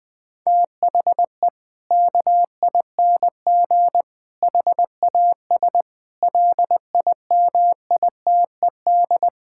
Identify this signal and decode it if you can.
Morse code, heard as a keyed tone, about 21 wpm, 700 Hz. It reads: THE KING HAS LIMITED
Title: Sentences from Top 1000 Words - Review Part 6 - 20wpm